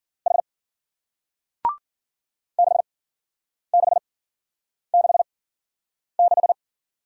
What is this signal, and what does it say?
Morse code, heard as a keyed tone, about 52 wpm, spaced as for 9 wpm, 700 Hz. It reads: H 6666